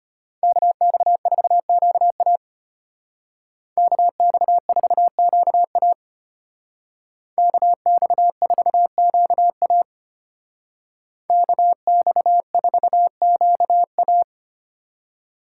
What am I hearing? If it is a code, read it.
Morse code, heard as a keyed tone, about 32 wpm, 700 Hz. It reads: KX4QA KX4QA KX4QA KX4QA